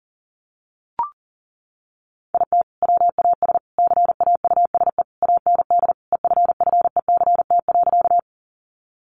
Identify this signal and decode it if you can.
Morse code, heard as a keyed tone, about 40 wpm, 700 Hz. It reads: IT WAS CAUSE AND EFFECT.
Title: Sentences from Top 1000 Words 40wpm